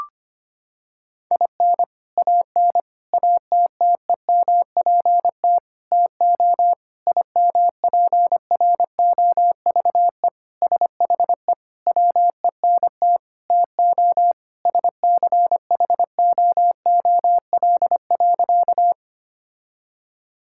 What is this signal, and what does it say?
Morse code, heard as a keyed tone, about 25 wpm, 700 Hz. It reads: IN AN ATTEMPT TO IMPROVE SHE WENT TO SCHOOL.
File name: practice / 25wpm / Sentences with 4-Word Phrases - 25wpm